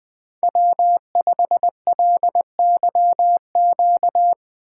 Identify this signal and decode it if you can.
Morse code, heard as a keyed tone, about 20 wpm, 700 Hz. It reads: W5LYQ
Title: Speed Racing - USA Call Signs 17wpm